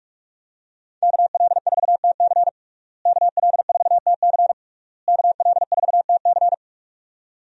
Morse code, heard as a keyed tone, about 46 wpm, 700 Hz. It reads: KL4TC KL4TC KL4TC
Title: USA Call Signs - 5 Characters - Repeated 3x - 45wpm